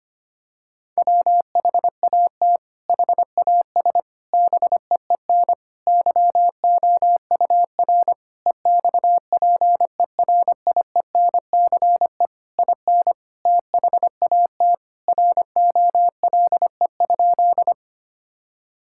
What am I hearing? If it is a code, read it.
Morse code, heard as a keyed tone, about 25 wpm, 700 Hz. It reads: WHAT HAS BEEN YOUR EXPERIENCE IN THAT ROLE?
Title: Sentences from Top 1000 Words - Review Part 6 - 25wpm